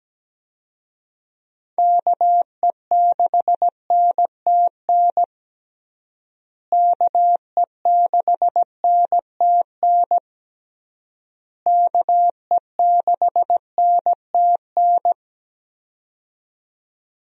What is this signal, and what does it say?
Morse code, heard as a keyed tone, about 17 wpm, 700 Hz. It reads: KE6NTN KE6NTN KE6NTN
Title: USA Call Signs - 6 Characters - Repeated 3x - 17wpm